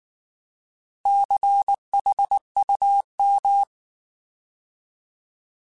Morse code, heard as a keyed tone, about 19 wpm, 800 Hz. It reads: CHUM